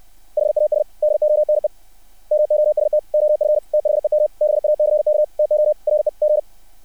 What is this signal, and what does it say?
Morse code, heard as a keyed tone, about 60 wpm, 600 Hz. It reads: HAS DONE MOST OF THEM BACK TO REG
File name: archived / CRSnet